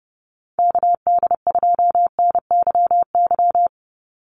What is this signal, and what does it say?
Morse code, heard as a keyed tone, about 30 wpm, 700 Hz. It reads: KD2NYY
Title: USA Call Signs 30wpm